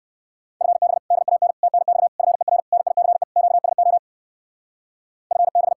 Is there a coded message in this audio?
Yes: Morse code from 0.6 to 5.7 s, heard as a keyed tone, about 62 wpm, 700 Hz. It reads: FB DAN TNX FER NICE QSO FB